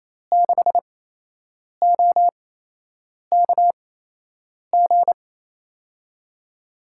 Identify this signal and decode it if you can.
Morse code, heard as a keyed tone, about 28 wpm, spaced as for 8 wpm, 700 Hz. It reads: 6OKG
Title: USA Call Signs - 8x Character Spacing - 28wpm